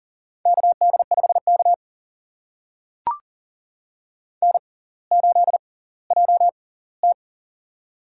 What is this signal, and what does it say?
Morse code, heard as a keyed tone, about 39 wpm, spaced as for 20 wpm, 690 Hz. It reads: KD5K N8JT
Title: USA Call Signs - 4 Characters - 6x Character Spacing - 40wpm